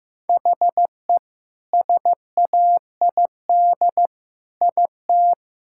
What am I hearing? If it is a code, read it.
Morse code, heard as a keyed tone, about 15 wpm, 700 Hz. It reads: HE SAID IT
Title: Sentences from Top 1600 Words 15wpm